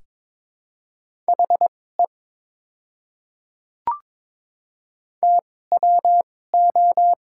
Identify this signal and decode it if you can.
Morse code, heard as a keyed tone, about 22 wpm, spaced as for 16 wpm, 690 Hz. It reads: HE TWO
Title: 2x Character Spacing - Top 100 Words - 22wpm